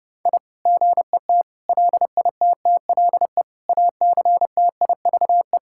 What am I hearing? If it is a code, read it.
Morse code, heard as a keyed tone, about 30 wpm, 700 Hz. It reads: I GET LITTLE ACTIVE
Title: Sentences from Top 1000 Words - Review Part 6 - 30wpm